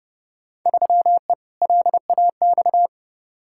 Morse code, heard as a keyed tone, about 30 wpm, 700 Hz. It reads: 3E LAX
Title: Field Day Exchanges Encore - 30wpm